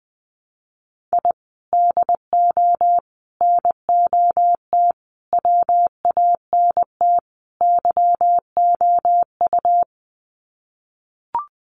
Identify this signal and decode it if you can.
Morse code, heard as a keyed tone, about 20 wpm, 700 Hz. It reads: I DO NOT WANT YOU
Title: Sets of 5 Words 20wpm